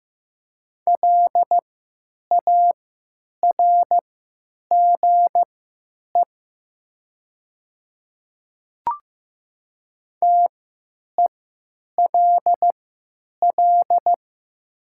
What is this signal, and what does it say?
Morse code, heard as a keyed tone, about 15 wpm, spaced as for 9 wpm, 700 Hz. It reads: LARGE TELL
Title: 3x Character Spacing - Top 300 Words - 15wpm